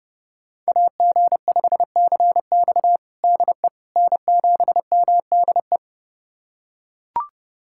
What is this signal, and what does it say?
Morse code, heard as a keyed tone, about 30 wpm, 700 Hz. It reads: AG5CX DE N7MDE
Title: USA Callsign DE Callsign - 5 Characters - 30wpm